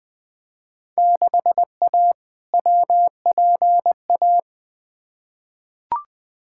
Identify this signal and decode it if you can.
Morse code, heard as a keyed tone, about 20 wpm, 700 Hz. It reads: 6A WPA
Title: Field Day Exchanges Encore - 20wpm